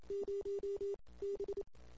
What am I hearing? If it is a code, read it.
Morse code, heard as a keyed tone, about 28 wpm, 400 Hz. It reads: 0 B
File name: CW / 2008-ss-cw-k5tr-op-n5rz -22:00